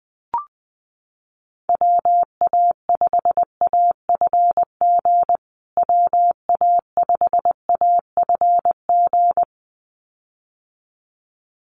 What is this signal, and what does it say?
Morse code, heard as a keyed tone, about 19 wpm, 700 Hz. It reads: WA5AFG WA5AFG
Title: USA Call Signs Repeated 2x 20wpm